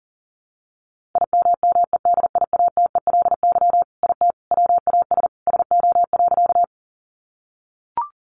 Morse code, heard as a keyed tone, about 37 wpm, 700 Hz. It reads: IMMEDIATELY IT WAS SO.
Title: Sentences from Top 1100 Words 40wpm